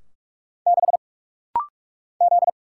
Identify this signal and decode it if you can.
Morse code, heard as a keyed tone, about 46 wpm, 700 Hz. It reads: 6 Z
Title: Single Letter-Number - Rapid-Fire 45wpm